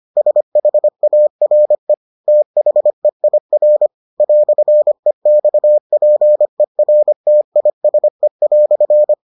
Morse code, heard as a keyed tone, about 25 wpm, 600 Hz. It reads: SHARE THEIR "EXPERTISE"